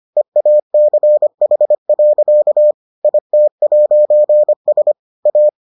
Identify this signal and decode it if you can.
Morse code, heard as a keyed tone, about 25 wpm, 600 Hz. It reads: EACH. IT'S A